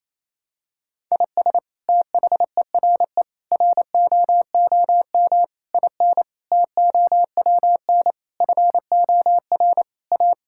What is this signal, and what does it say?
Morse code, heard as a keyed tone, about 28 wpm, 700 Hz. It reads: IS THERE ROOM IN TOWN FOR A
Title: Sentences from Top 1000 Words - Review Part 5 - 28wpm